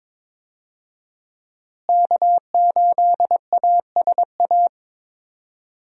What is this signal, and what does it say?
Morse code, heard as a keyed tone, about 22 wpm, 700 Hz. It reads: K8ASA
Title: USA Call Signs - 5 Characters - Repeated 3x - 22wpm